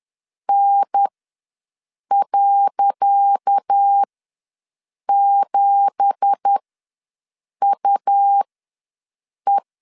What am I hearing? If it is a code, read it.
Morse code, heard as a keyed tone, about 10 wpm, spaced as for 6 wpm, 800 Hz. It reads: N.7UE
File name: LEZIONI / 10 SIGNS / 10C DOT ws 10wpm 4213